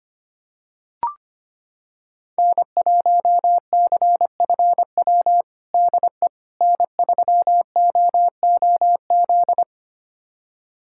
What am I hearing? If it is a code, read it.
Morse code, heard as a keyed tone, about 25 wpm, 700 Hz. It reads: N1CFW DE N3OOZ
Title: USA Callsign DE Callsign 25wpm